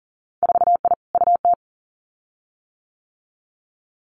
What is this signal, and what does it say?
Morse code, heard as a keyed tone, about 42 wpm, 700 Hz. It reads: 4I UT
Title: Winter Field Day Exchanges 40wpm